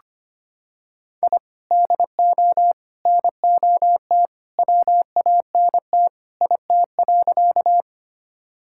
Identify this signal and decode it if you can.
Morse code, heard as a keyed tone, about 25 wpm, 700 Hz. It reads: I DO NOT WANT IT.
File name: practice / 25wpm / Sentences with 2-Word Phrases - 25wpm